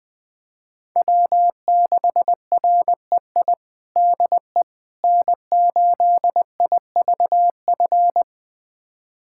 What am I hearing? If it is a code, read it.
Morse code, heard as a keyed tone, about 20 wpm, 700 Hz. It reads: W6REI DE N8IVF
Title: USA Callsign DE Callsign 20wpm